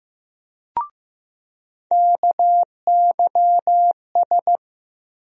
Morse code, heard as a keyed tone, about 15 wpm, 700 Hz. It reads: KYS
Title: USA Call Sign Postfixes - 3 Letters 15wpm